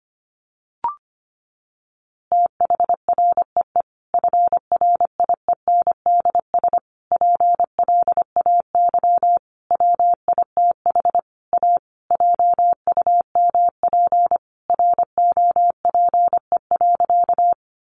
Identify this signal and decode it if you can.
Morse code, heard as a keyed tone, about 25 wpm, 700 Hz. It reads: THREE FRIENDS PLAY WITH A JUMP ROPE.
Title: Easy Phrases 25wpm